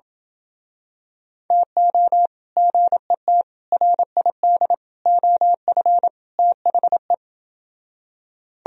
Morse code, heard as a keyed tone, about 28 wpm, 700 Hz. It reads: TO GET RID OF THE